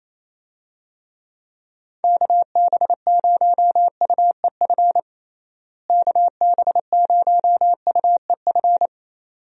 Morse code, heard as a keyed tone, about 27 wpm, 700 Hz. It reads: KB0UEF KB0UEF
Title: USA Call Signs - 6 Characters - Repeated 2x - 28wpm